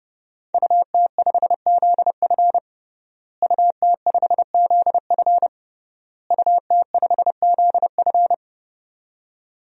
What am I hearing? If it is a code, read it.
Morse code, heard as a keyed tone, about 30 wpm, 700 Hz. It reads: UT5ZF UT5ZF UT5ZF
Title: DXCC-100 Call Signs - 5 Characters - Repeated 3x - 30wpm